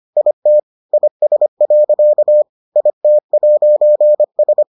Morse code, heard as a keyed tone, about 25 wpm, 600 Hz. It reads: IT IS. IT'S